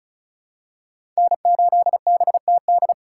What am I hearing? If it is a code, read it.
Morse code, heard as a keyed tone, about 36 wpm, 700 Hz. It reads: N8BTD
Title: USA Call Signs - 3x Character Spacing - 35wpm